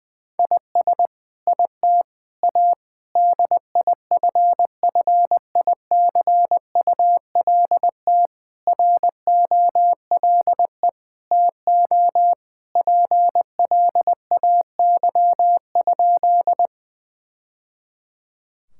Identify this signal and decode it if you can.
Morse code, heard as a keyed tone, about 20 wpm, 700 Hz. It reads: IS IT A DIFFICULT ROLE TO PLAY?
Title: Sentences from Top 1000 Words - Review Part 6 - 20wpm